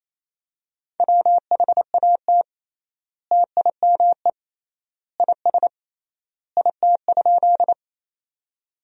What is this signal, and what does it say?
Morse code, heard as a keyed tone, about 28 wpm, 700 Hz. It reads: WHAT TIME IS IT?
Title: 3x Word-Spacing - Sentences from Top 100 Words - 28wpm